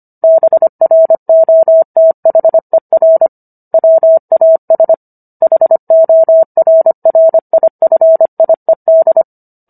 Morse code, heard as a keyed tone, about 25 wpm, 650 Hz. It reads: BROTHER WAS HORRIFIED